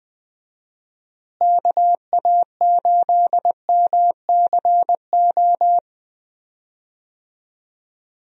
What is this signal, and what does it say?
Morse code, heard as a keyed tone, about 20 wpm, 700 Hz. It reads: KA8MCO
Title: USA Call Signs - 6 Characters - Repeated 3x - 20wpm